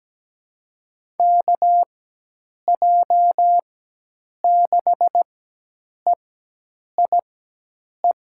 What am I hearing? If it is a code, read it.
Morse code, heard as a keyed tone, about 17 wpm, spaced as for 8 wpm, 700 Hz. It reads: KJ6EIE
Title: USA Call Signs - 6 Characters - 4x Character Spacing - 17wpm